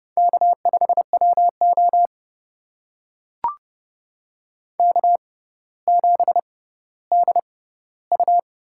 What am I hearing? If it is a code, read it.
Morse code, heard as a keyed tone, about 30 wpm, spaced as for 15 wpm, 700 Hz. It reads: K5WO K7DU
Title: USA Call Signs - 4 Characters - 6x Character Spacing - 30wpm